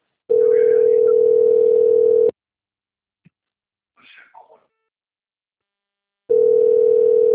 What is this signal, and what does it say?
Ringback tone (North America), 440 + 480 Hz.